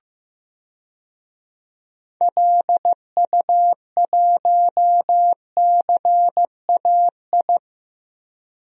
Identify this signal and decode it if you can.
Morse code, heard as a keyed tone, about 15 wpm, 700 Hz. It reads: LU1CAI
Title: DXCC-100 Call Signs - 6 Characters - Repeated 3x - 15wpm